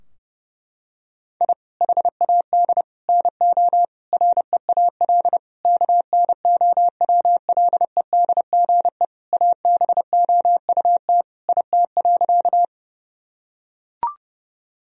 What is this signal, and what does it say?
Morse code, heard as a keyed tone, about 30 wpm, 700 Hz. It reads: I HAD NO REAL KNOWLEDGE ABOUT IT.
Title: Sentences from Top 1000 Words - Review Part 3 - 30wpm